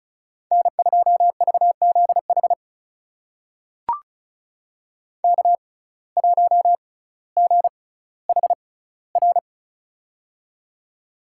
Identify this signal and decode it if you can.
Morse code, heard as a keyed tone, about 36 wpm, spaced as for 18 wpm, 700 Hz. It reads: N2VZH K1GHR
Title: USA Call Signs - 5 Characters - 6x Character Spacing - 35wpm